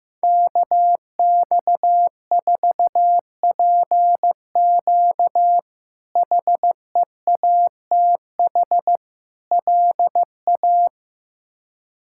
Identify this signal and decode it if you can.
Morse code, heard as a keyed tone, about 16 wpm, 700 Hz. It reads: KX4PQ HEATH LA
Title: K1USN SST QSO Practice - 15wpm